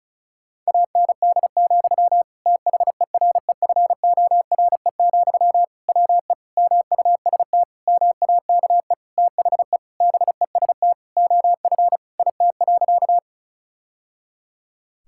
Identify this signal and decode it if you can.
Morse code, heard as a keyed tone, about 36 wpm, 700 Hz. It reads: AND, THEREFORE, WE MUST MAKE THE BEST OF IT.
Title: Sentences from Top 1000 Words - Review Part 5 - 35wpm